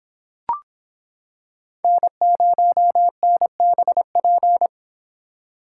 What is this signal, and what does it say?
Morse code, heard as a keyed tone, about 26 wpm, 700 Hz. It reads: N0NBP